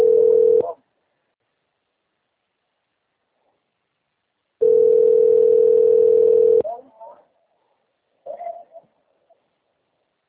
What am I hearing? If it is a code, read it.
Ringback tone (North America), 440 + 480 Hz, with speech.
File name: out- - - - - .63657